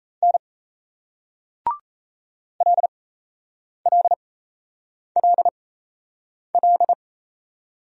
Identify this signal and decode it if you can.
Morse code, heard as a keyed tone, about 38 wpm, spaced as for 8 wpm, 700 Hz. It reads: N LLLL